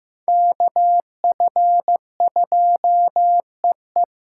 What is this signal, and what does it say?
Morse code, heard as a keyed tone, about 16 wpm, 700 Hz. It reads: KF2EE
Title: USA Callsign DE Callsign - 5 Characters - 15wpm